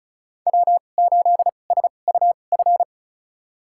Morse code, heard as a keyed tone, about 34 wpm, spaced as for 25 wpm, 700 Hz. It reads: W8SUF
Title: USA Call Signs - 5 Characters - 2x Character Spacing - 35wpm